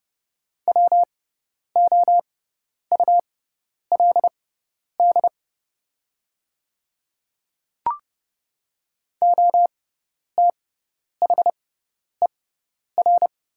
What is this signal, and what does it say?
Morse code, heard as a keyed tone, about 30 wpm, spaced as for 10 wpm, 700 Hz. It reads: WOULD OTHER
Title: 6x Character Spacing - Top 200 Words - 30wpm